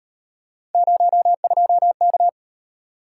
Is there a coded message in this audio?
Yes: Morse code from 0.7 to 2.3 s, heard as a keyed tone, about 36 wpm, 700 Hz. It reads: 02K